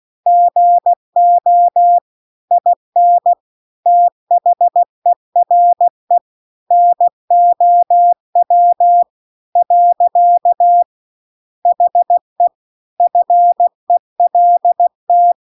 Morse code, heard as a keyed tone, about 16 wpm, 700 Hz. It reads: GO IN THERE NOW . HE FELT